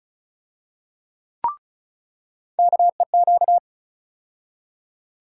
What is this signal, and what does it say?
Morse code, heard as a keyed tone, about 34 wpm, 700 Hz. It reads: KEQ
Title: USA Call Sign Postfixes - 3 Letters 35wpm